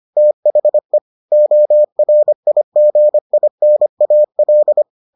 Morse code, heard as a keyed tone, about 25 wpm, 600 Hz. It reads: THE ORIGINAL